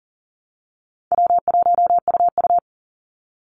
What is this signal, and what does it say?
Morse code, heard as a keyed tone, about 40 wpm, 700 Hz. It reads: W1UU
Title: Easy Call Signs Repeated 2x - 40wpm